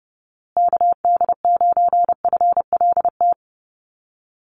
Morse code, heard as a keyed tone, about 30 wpm, 700 Hz. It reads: KD9FLT
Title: USA Call Signs Repeated 3x 30wpm